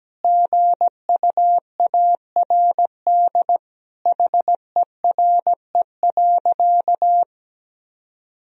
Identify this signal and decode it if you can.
Morse code, heard as a keyed tone, about 17 wpm, 700 Hz. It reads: GUARD HERE.
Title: Sentences from Top 1000 Words - Review Part 4 - 17wpm